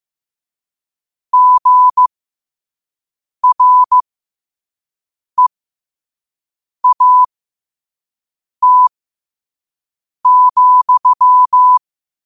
Morse code, heard as a keyed tone, about 15 wpm, spaced as for 5 wpm, 1000 Hz. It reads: GREAT,
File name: qotdfiles / 5wpm